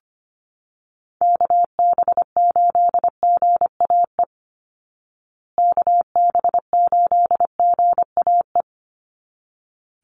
Morse code, heard as a keyed tone, about 25 wpm, 700 Hz. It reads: KB8GAE KB8GAE
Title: Call Signs Repeated 3x - 25wpm